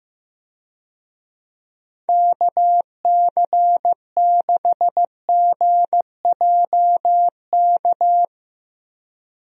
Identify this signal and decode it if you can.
Morse code, heard as a keyed tone, about 15 wpm, 700 Hz. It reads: KC6GJK
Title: USA Call Signs - 6 Characters - Repeated 2x - 15wpm